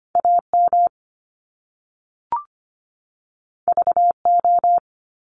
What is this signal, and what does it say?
Morse code, heard as a keyed tone, about 25 wpm, 700 Hz. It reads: AM VO
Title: Top Two Letter Combinations in Words 25wpm